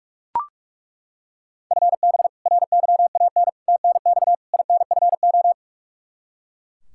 Morse code, heard as a keyed tone, about 41 wpm, 700 Hz. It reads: FB RYAN TNX INFO
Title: K1USN SST QSO Practice 45wpm